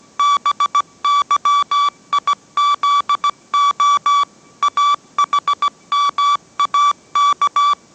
Morse code, heard as a keyed tone, about 17 wpm, 1150 Hz. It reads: BYIZO AHMAK